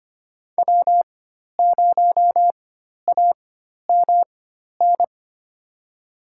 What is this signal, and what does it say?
Morse code, heard as a keyed tone, about 25 wpm, spaced as for 12 wpm, 700 Hz. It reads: W0AMN
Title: USA Call Signs - 5 Characters - 4x Character Spacing - 25wpm